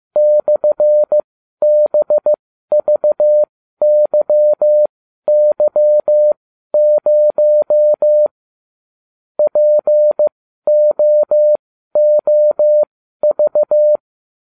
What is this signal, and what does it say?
Morse code, heard as a keyed tone, about 15 wpm, spaced as for 12 wpm, 600 Hz. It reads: /BVYY0 POOV